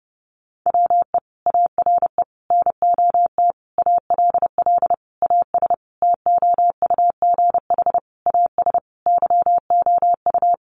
Morse code, heard as a keyed tone, about 30 wpm, 700 Hz. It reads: WE ARE NOT ALL AS TOUGH AS YOU
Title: Sentences from Top 1200 Words 30wpm